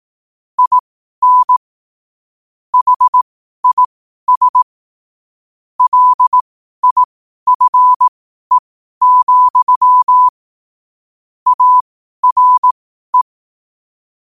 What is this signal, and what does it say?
Morse code, heard as a keyed tone, about 18 wpm, spaced as for 12 wpm, 1000 Hz. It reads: IN HIS LIFE, ARE
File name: qotdfiles / 12wpm